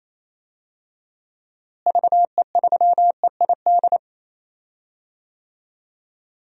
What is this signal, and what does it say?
Morse code, heard as a keyed tone, about 27 wpm, 700 Hz. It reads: VE3EID
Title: DXCC-100 Call Signs - 6 Characters - Repeated 3x - 28wpm